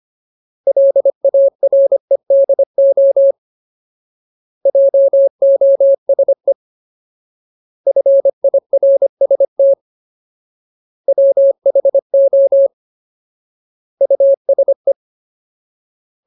Morse code, heard as a keyed tone, about 25 wpm, 550 Hz. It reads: LAREDO JOSE FIRST WHO USE
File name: Ultimate Word List / 25WPM Ultimate